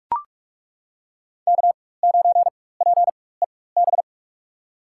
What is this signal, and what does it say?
Morse code, heard as a keyed tone, about 42 wpm, spaced as for 21 wpm, 700 Hz. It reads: K9PEB
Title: USA Call Signs - 5 Characters - 4x Character Spacing - 45wpm